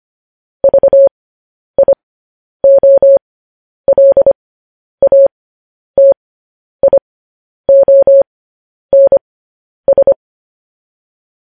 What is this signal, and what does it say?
Morse code, heard as a keyed tone, about 25 wpm, spaced as for 10 wpm, 550 Hz. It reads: VIOLATIONS
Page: News Headlines in Morse Code at 10 WPM